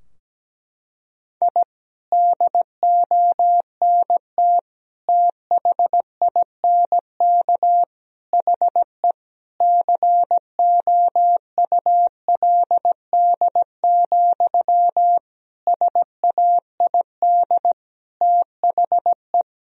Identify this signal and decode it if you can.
Morse code, heard as a keyed tone, about 17 wpm, 700 Hz. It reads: I DONT THINK HE COULD, SAID THE
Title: Sentences from Top 1000 Words - Review Part 4 - 17wpm